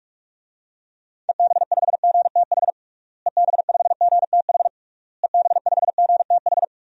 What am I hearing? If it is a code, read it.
Morse code, heard as a keyed tone, about 42 wpm, 700 Hz. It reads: EB5GTH EB5GTH EB5GTH
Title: DXCC-100 Call Signs - 6 Characters - Repeated 3x - 45wpm